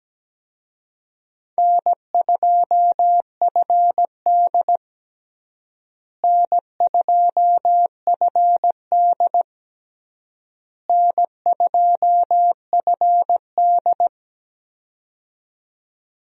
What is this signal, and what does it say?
Morse code, heard as a keyed tone, about 17 wpm, 700 Hz. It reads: N2FD N2FD N2FD
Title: USA Call Signs - 4 Characters - Repeated 3x - 17wpm